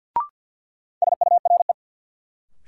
Morse code, heard as a keyed tone, about 52 wpm, 700 Hz. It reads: SURE
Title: Sentences from Top 300 Words - New Word Drill - 50wpm